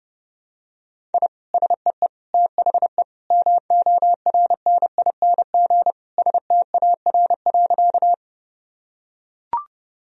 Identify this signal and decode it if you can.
Morse code, heard as a keyed tone, about 32 wpm, 700 Hz. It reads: I SEE THE MORNING STAR.
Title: Sentences from Top 1000 Words - Review Part 2 - 30wpm